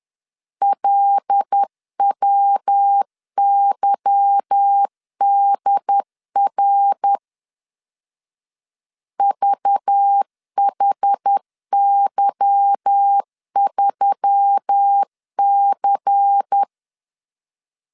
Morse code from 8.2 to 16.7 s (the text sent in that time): VHY3C